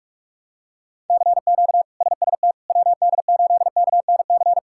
Morse code, heard as a keyed tone, about 46 wpm, 700 Hz. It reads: CQ SST WD8KNC
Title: K1USN SST QSO Practice - 45wpm